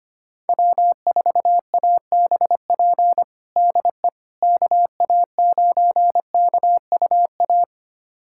Morse code, heard as a keyed tone, about 25 wpm, 700 Hz. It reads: W4ABP DE KA9KUA
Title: USA Callsign DE Callsign 25wpm